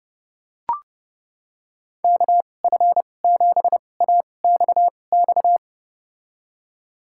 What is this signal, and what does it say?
Morse code, heard as a keyed tone, about 30 wpm, spaced as for 22 wpm, 700 Hz. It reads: KF7AXX
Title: USA Call Signs - 6 Characters - 2x Character Spacing - 30wpm